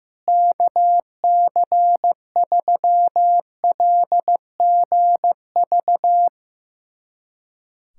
Morse code, heard as a keyed tone, about 16 wpm, 700 Hz. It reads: KC3LGV
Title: USA Callsign DE Callsign - 6 Characters - 15wpm